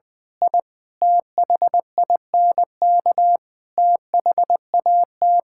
Morse code, heard as a keyed tone, about 20 wpm, 690 Hz. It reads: I THINK THAT
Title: Build A Phrase - 4-Word Phrases - 20wpm